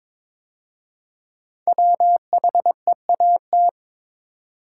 Morse code, heard as a keyed tone, about 22 wpm, 700 Hz. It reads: WHEAT
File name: 5 Letter Words - First 3 Letters Given - 22wpm